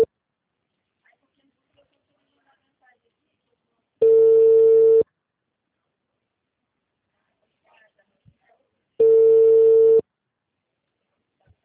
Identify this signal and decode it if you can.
Ringback tone (China, 450 Hz), with speech.